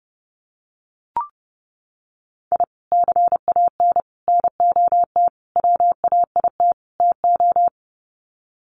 Morse code, heard as a keyed tone, about 30 wpm, 700 Hz. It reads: I CAN NOT WAIT TO
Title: Sets of 5 Words Encore 30wpm